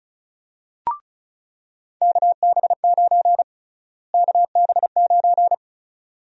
Morse code, heard as a keyed tone, about 36 wpm, 700 Hz. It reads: KB9 KB9
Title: USA Call Sign Prefixes - Repeated 2x - 35wpm